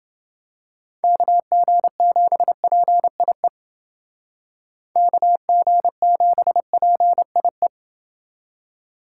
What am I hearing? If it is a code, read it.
Morse code, heard as a keyed tone, about 29 wpm, 700 Hz. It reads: KG7PIE KG7PIE